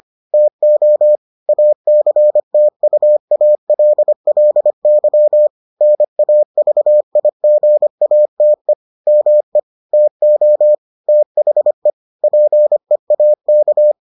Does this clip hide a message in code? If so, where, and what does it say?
Yes: Morse code from 0.3 to 13.9 s, heard as a keyed tone, about 25 wpm, 600 Hz. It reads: TO ACTUALLY NAVIGATE ME TO THE PEAK